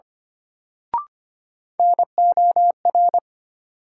Morse code, heard as a keyed tone, about 25 wpm, 700 Hz. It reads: NOR
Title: Sentences from Top 1500 Words - New Word Drill Alt - 25wpm